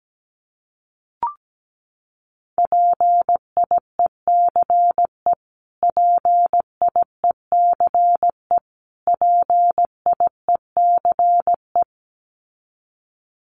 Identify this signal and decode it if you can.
Morse code, heard as a keyed tone, about 17 wpm, 700 Hz. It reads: PIECE PIECE PIECE
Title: Top 500 Words Repeated 17wpm